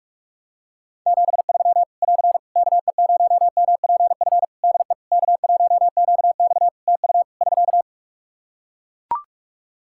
Morse code, heard as a keyed tone, about 41 wpm, 700 Hz. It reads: 73 <AR> KE0MPF DE K1QX TU <SK>
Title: Practice QSO - Exchange 4 - Ending - 45wpm